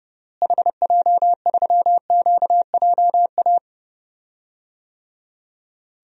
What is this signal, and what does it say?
Morse code, heard as a keyed tone, about 30 wpm, 700 Hz. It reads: HJ3QJA
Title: DXCC-100 Call Signs - 6 Characters - Repeated 2x - 30wpm